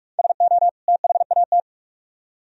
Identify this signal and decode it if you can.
Morse code, heard as a keyed tone, about 58 wpm, 700 Hz. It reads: SO THAT